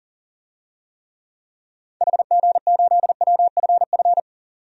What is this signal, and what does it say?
Morse code, heard as a keyed tone, about 40 wpm, 700 Hz. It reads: HG8WFF
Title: DXCC-100 Call Signs - 6 Characters - 40wpm